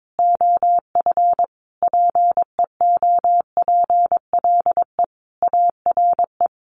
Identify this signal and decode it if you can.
Morse code, heard as a keyed tone, about 22 wpm, 700 Hz. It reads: OF PEOPLE ARE
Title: Sets of 5 Words 22wpm